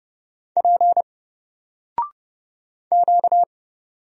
Morse code, heard as a keyed tone, about 30 wpm, 700 Hz. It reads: P Q
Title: Single Character (Letter:Number:Punctuation:Prosign) 30wpm